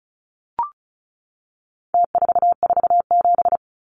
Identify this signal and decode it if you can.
Morse code, heard as a keyed tone, about 34 wpm, 700 Hz. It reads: T447
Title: Numbers - Contest Style - 4-Digit 35wpm